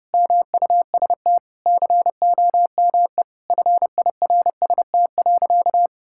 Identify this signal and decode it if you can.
Morse code, heard as a keyed tone, about 30 wpm, 700 Hz. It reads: MUST COME FIRST.
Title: Sentences from Top 1000 Words - Review Part 6 - 30wpm